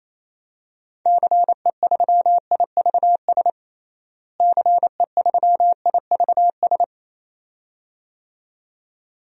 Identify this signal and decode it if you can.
Morse code, heard as a keyed tone, about 28 wpm, 700 Hz. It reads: CE3IVS CE3IVS
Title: DXCC-100 Call Signs - 6 Characters - Repeated 2x - 28wpm